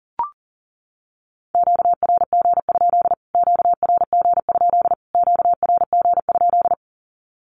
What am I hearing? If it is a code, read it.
Morse code, heard as a keyed tone, about 40 wpm, 700 Hz. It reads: QRG? QRG? QRG?
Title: Q-Codes Repeated 3x 40wpm